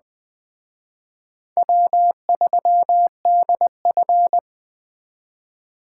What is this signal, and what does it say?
Morse code, heard as a keyed tone, about 21 wpm, 700 Hz. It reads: W3DF
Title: Speed Racing - USA 1x2 Call Signs 17wpm